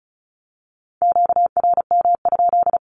Morse code, heard as a keyed tone, about 34 wpm, 700 Hz. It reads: QRM?
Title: Q-Codes 35wpm